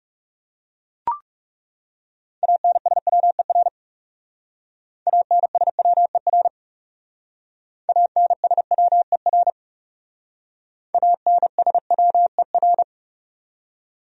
Morse code, heard as a keyed tone, about 37 wpm, 700 Hz. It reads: ANSWER ANSWER ANSWER ANSWER
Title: Speed-Racing Top 500 Words 30wpm